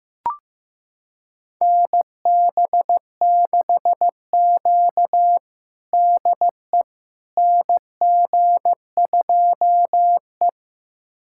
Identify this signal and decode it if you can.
Morse code, heard as a keyed tone, about 15 wpm, 700 Hz. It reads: NB6Q DE NG2E
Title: USA Callsign DE Callsign - 4 Characters - 15wpm